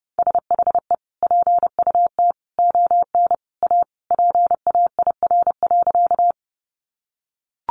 Morse code, heard as a keyed tone, about 30 wpm, 700 Hz. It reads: SHE PUT ON A PAIR.
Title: Sentences from Top 1200 Words 30wpm